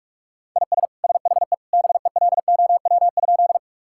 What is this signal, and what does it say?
Morse code, heard as a keyed tone, about 41 wpm, 690 Hz. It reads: IS SHE BELOW?
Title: Sentences from Top 1000 Words - Review Part 5 - 45wpm